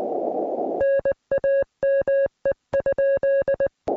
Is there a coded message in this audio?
Yes: Morse code from 0.8 to 3.7 s, heard as a keyed tone, about 20 wpm, 1650 Hz. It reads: NAME?